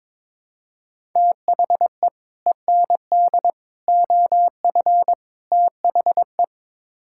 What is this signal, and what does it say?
Morse code, heard as a keyed tone, about 22 wpm, 700 Hz. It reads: THE END OF THE
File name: practice / 22wpm / Build A Phrase - 5-Word Phrases - 22wpm